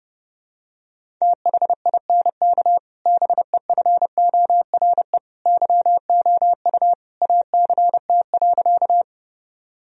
Morse code, heard as a keyed tone, about 30 wpm, 700 Hz. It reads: THINK BEFORE YOU ACT.
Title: Sentences from Top 1000 Words - Review Part 1 - 30wpm